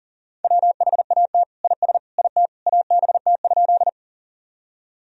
Morse code, heard as a keyed tone, about 40 wpm, 700 Hz. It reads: WHAT IS IT ABT?
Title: Sentences from Top 100 Words and Abbreviations 40wpm